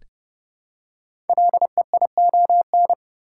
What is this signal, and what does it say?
Morse code, heard as a keyed tone, about 30 wpm, 700 Hz. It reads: LEION
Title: Code Groups - 5 Letter:Numbers - 30wpm